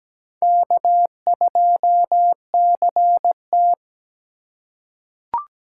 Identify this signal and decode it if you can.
Morse code, heard as a keyed tone, about 17 wpm, 700 Hz. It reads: K2CT
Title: USA Call Signs - 4 Characters - 8x Character Spacing - 17wpm